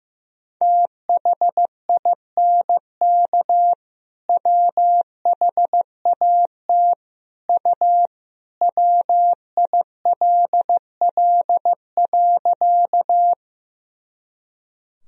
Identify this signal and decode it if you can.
Morse code, heard as a keyed tone, about 15 wpm, 700 Hz. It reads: THINK WHAT U WILL.
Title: Sentences from Top 200 Words and Abbreviations 15wpm